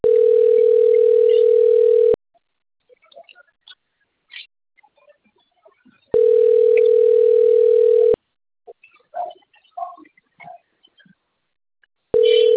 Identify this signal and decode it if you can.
Ringback tone (North America), 440 + 480 Hz.